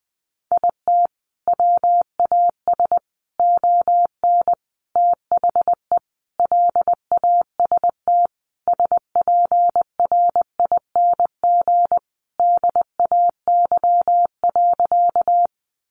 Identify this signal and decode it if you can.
Morse code, heard as a keyed tone, about 20 wpm, 700 Hz. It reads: IT WAS ON THE LAST SPRING DAY.
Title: Sentences from Top 1200 Words 20wpm